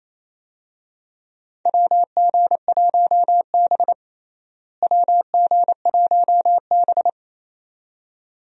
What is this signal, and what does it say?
Morse code, heard as a keyed tone, about 28 wpm, 700 Hz. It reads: WG1B WG1B
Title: USA Call Signs - 4 Characters - Repeated 2x - 28wpm